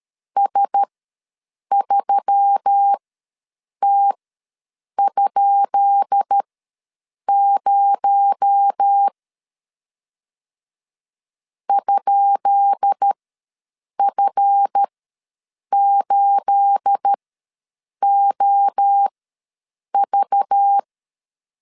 Morse code, heard as a keyed tone, about 13 wpm, spaced as for 7 wpm, 800 Hz. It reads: S3T?0 ?F8OV
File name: LEZIONI / 10 SIGNS / 10B INT ws 12wpm 3527